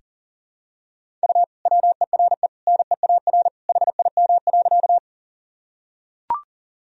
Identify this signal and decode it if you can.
Morse code, heard as a keyed tone, about 37 wpm, 700 Hz. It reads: U WERE NEAR HIM.
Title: Sentences from Top 200 Words and Abbreviations 40wpm